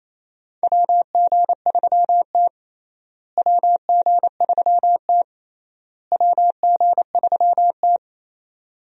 Morse code, heard as a keyed tone, about 28 wpm, 700 Hz. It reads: WG3T WG3T WG3T
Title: USA Call Signs - 4 Characters - Repeated 3x - 28wpm